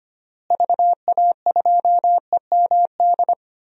Morse code, heard as a keyed tone, about 25 wpm, 700 Hz. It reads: VA2EMD
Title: DXCC-100 Call Signs - 6 Characters - Repeated 3x - 25wpm